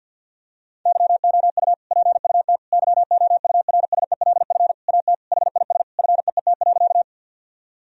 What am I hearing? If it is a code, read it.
Morse code, heard as a keyed tone, about 51 wpm, 700 Hz. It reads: YOU PUT YOURSELF AT HIS FEET.
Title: Sentences from Top 1000 Words - Review Part 1 - 50wpm